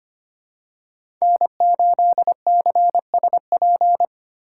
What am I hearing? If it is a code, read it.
Morse code, heard as a keyed tone, about 25 wpm, 700 Hz. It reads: N8CSP
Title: USA Call Signs - 5 Characters - 3x Character Spacing - 25wpm